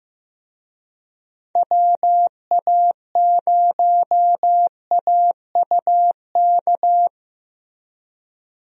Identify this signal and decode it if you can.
Morse code, heard as a keyed tone, about 15 wpm, 700 Hz. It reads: WA0AUK
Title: USA Call Signs - 6 Characters - Repeated 2x - 15wpm